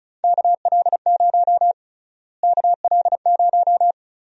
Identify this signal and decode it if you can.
Morse code, heard as a keyed tone, about 36 wpm, 700 Hz. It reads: KL0 KL0
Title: USA Call Sign Prefixes - Repeated 2x - 35wpm